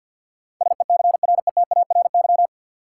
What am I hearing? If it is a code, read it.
Morse code, heard as a keyed tone, about 50 wpm, 700 Hz. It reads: SECRETARY
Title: Sentences from Top 1300 Words - New Word Drill Alt - 50wpm